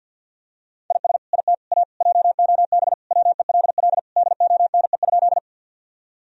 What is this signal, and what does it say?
Morse code, heard as a keyed tone, about 52 wpm, 700 Hz. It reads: IS IT A JOB WELL DONE?
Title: Sentences from Top 1000 Words - Review Part 3 - 50wpm